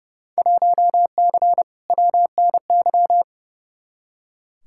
Morse code, heard as a keyed tone, about 30 wpm, 700 Hz. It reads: 1C WNY
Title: Field Day Exchanges Encore - 30wpm